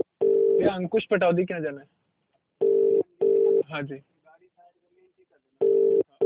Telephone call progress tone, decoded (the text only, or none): ringback tone